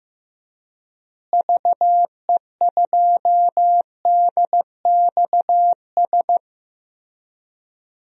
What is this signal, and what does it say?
Morse code, heard as a keyed tone, about 15 wpm, 700 Hz. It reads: VE2DXS
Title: DXCC-100 Call Signs - 6 Characters - Repeated 3x - 15wpm